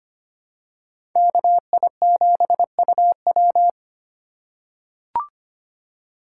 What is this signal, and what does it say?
Morse code, heard as a keyed tone, about 25 wpm, 700 Hz. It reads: KI7UW
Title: USA Call Signs - 5 Characters - 3x Character Spacing - 25wpm